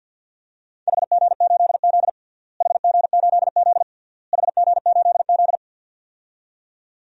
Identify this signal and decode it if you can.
Morse code, heard as a keyed tone, about 50 wpm, 700 Hz. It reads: HG8Z HG8Z HG8Z
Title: DXCC-100 Call Signs - 4 Characters - Repeated 3x - 50wpm